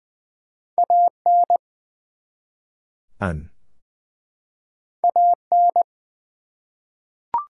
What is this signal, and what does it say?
Morse code, heard as a keyed tone, about 20 wpm, 700 Hz. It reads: AN AN
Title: Two Letter Words - 20wpm